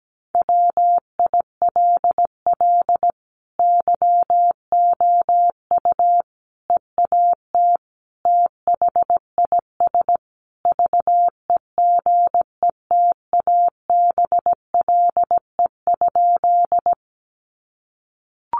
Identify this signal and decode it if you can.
Morse code, heard as a keyed tone, about 17 wpm, 700 Hz. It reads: WILL YOU EAT THIS VEGETABLE?
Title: Sentences from Top 1300 Words 17wpm